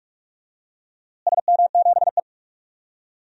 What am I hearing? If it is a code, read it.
Morse code, heard as a keyed tone, about 47 wpm, 700 Hz. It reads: SM7E
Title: DXCC-100 Call Signs - 4 Characters - Repeated 3x - 45wpm